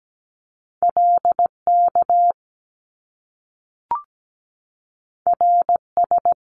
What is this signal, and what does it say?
Morse code, heard as a keyed tone, about 17 wpm, 700 Hz. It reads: LK RS
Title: Top Two Letter Combinations in Words 17wpm